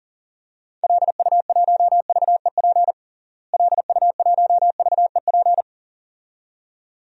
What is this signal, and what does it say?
Morse code, heard as a keyed tone, about 40 wpm, 700 Hz. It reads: LU1VEP LU1VEP
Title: DXCC-100 Call Signs - 6 Characters - Repeated 2x - 40wpm